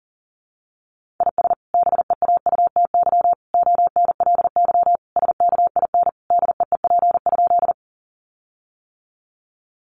Morse code, heard as a keyed tone, about 40 wpm, 700 Hz. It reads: IS BEAUTY ONLY SKIN DEEP?
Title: Sentences from Top 1300 Words 40wpm